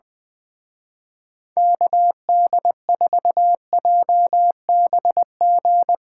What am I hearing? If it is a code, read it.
Morse code, heard as a keyed tone, about 20 wpm, 700 Hz. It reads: KD4JBG
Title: Speed Racing - USA 2x3 Call Signs 17wpm